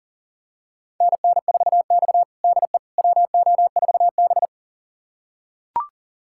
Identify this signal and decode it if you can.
Morse code, heard as a keyed tone, about 40 wpm, 700 Hz. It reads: NN4X DE WO4B
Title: USA Callsign DE Callsign - 4 Characters - 40wpm